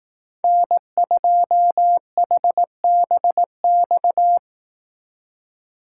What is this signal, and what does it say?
Morse code, heard as a keyed tone, about 18 wpm, 700 Hz. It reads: N2HBX
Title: Speed Racing - USA 1x3 Call Signs 15wpm